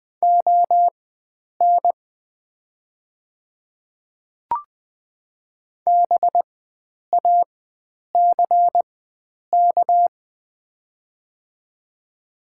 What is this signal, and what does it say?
Morse code, heard as a keyed tone, about 20 wpm, spaced as for 9 wpm, 700 Hz. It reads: ON BACK
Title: 4x Character Spacing - Top 300 Words - 20wpm